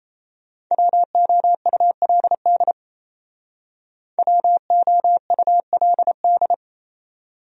Morse code, heard as a keyed tone, about 30 wpm, 700 Hz. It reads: WOULD WOULD
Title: Speed-Racing Top 100 Words 28wpm